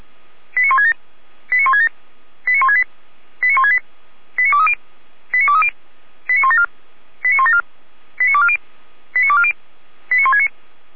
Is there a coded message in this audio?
Yes: five-tone selective call (ZVEI) from 0.6 to 10.5 s, heard as quick runs of five tones, tones of about 70 ms. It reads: 89167 89167 89168 89168 89230 89230 89174 89174 89240 89240 89169